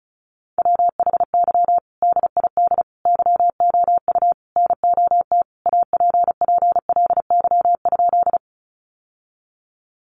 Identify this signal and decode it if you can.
Morse code, heard as a keyed tone, about 34 wpm, 700 Hz. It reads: WHY DID YOU NOT APPLY?
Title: Sentences from Top 1200 Words 35wpm